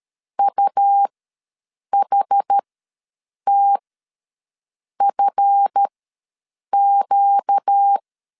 Morse code, heard as a keyed tone, about 13 wpm, spaced as for 7 wpm, 800 Hz. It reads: UHTFQ
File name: LEZIONI / 7 GF9Z / 7 GF9 ws 12wpm 3344